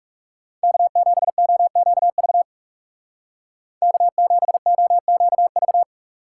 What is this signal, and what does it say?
Morse code, heard as a keyed tone, about 37 wpm, 700 Hz. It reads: K7OQV K7OQV
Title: Speed Racing - USA 1x3 Call Signs 30wpm